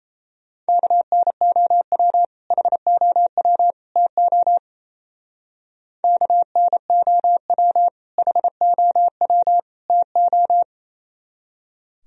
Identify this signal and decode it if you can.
Morse code, heard as a keyed tone, about 36 wpm, 700 Hz. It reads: KNOW HOW TO KNOW HOW TO